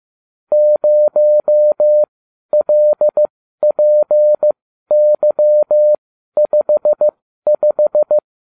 Morse code, heard as a keyed tone, about 15 wpm, spaced as for 12 wpm, 600 Hz. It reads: 0LPY55